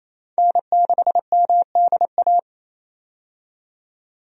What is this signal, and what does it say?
Morse code, heard as a keyed tone, about 28 wpm, 700 Hz. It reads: N6MDA
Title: USA Call Signs - 5 Characters - 28wpm